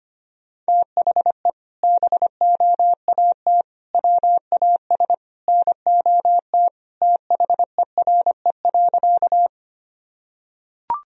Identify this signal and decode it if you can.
Morse code, heard as a keyed tone, about 25 wpm, 700 Hz. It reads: THE BOAT WAS NOT THERE.
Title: Sentences from Top 1000 Words - Review Part 3 - 25wpm